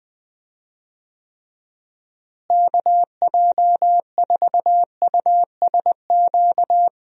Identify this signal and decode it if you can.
Morse code, heard as a keyed tone, about 20 wpm, 700 Hz. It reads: KJ4USQ
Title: USA Call Signs - 6 Characters - Repeated 2x - 20wpm